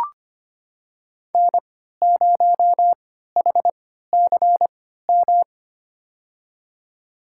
Morse code, heard as a keyed tone, about 25 wpm, spaced as for 14 wpm, 700 Hz. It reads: N0HCM
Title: USA Call Signs - 5 Characters - 3x Character Spacing - 25wpm